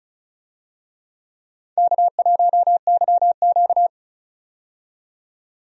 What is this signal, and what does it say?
Morse code, heard as a keyed tone, about 35 wpm, 700 Hz. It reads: K1YQ
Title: USA Call Signs - 4 Characters - 35wpm